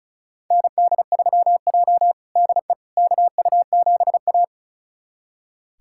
Morse code, heard as a keyed tone, about 36 wpm, 700 Hz. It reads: ND3J DE KU7A
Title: USA Callsign DE Callsign - 4 Characters - 35wpm